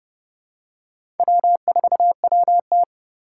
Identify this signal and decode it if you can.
Morse code, heard as a keyed tone, about 30 wpm, 700 Hz. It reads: W4WT
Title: USA Call Signs - 4x Character Spacing - 30wpm